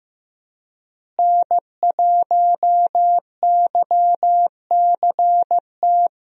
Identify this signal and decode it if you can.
Morse code, heard as a keyed tone, about 15 wpm, 700 Hz. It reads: N1YCT
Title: USA Call Signs - 5 Characters - Repeated 3x - 15wpm